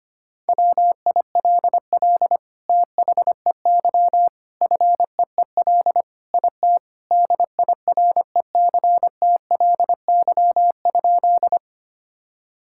Morse code, heard as a keyed tone, about 25 wpm, 700 Hz. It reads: WILL THEY FEEL IT DIRECTLY?
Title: Sentences from Top 1000 Words - Review Part 5 - 25wpm